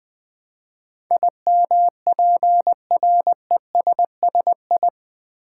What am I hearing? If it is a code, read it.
Morse code, heard as a keyed tone, about 20 wpm, 700 Hz. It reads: IMPRESSI
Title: Build A Word - 4 Letters At A Time - 10-Letter Words - 20wpm